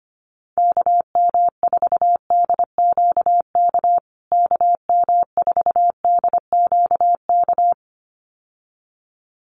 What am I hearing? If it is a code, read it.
Morse code, heard as a keyed tone, about 25 wpm, 700 Hz. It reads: KM4DQK KM4DQK
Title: USA Call Signs Repeated 2x 25wpm